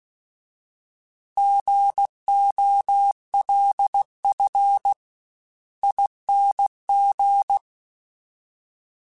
Morse code, heard as a keyed tone, about 16 wpm, 800 Hz. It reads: GOLF ING